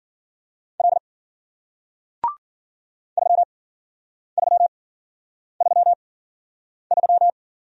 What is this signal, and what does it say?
Morse code, heard as a keyed tone, about 53 wpm, spaced as for 9 wpm, 700 Hz. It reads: L 3333